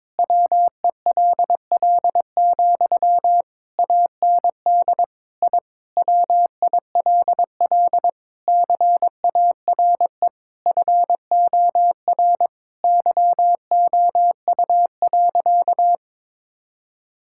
Morse code, heard as a keyed tone, about 21 wpm, 700 Hz. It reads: WELL, AND I WILL CARE FOR YOU.
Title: Sentences from Top 1000 Words - Review Part 6 - 22wpm